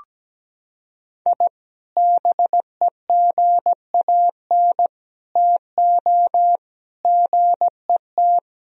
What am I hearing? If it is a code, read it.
Morse code, heard as a keyed tone, about 17 wpm, 700 Hz. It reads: I BEGAN TO GET
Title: Sentences from Top 1000 Words - Review Part 4 - 17wpm